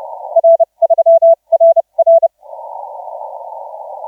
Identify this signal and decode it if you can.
Morse code, heard as a keyed tone, about 29 wpm, spaced as for 24 wpm, 660 Hz. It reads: R3RR
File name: 7S6W 0740UTC 17m